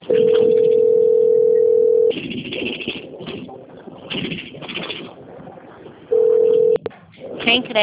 Ringback tone (North America), 440 + 480 Hz.
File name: out- - - - - .31075